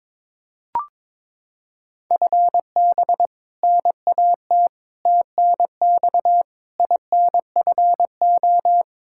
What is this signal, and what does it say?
Morse code, heard as a keyed tone, about 22 wpm, 700 Hz. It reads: FB NAT TNX INFO
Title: Practice QSO - Exchange 3 - More Info - 22wpm